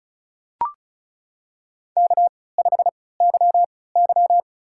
Morse code, heard as a keyed tone, about 34 wpm, spaced as for 20 wpm, 700 Hz. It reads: K5YY
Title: USA Call Signs - 4 Characters - 3x Character Spacing - 35wpm